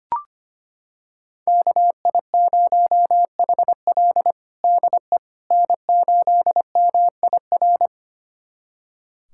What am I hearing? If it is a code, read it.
Morse code, heard as a keyed tone, about 25 wpm, 700 Hz. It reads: KI0HL DE N8MIR
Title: USA Callsign DE Callsign - 5 Characters - 25wpm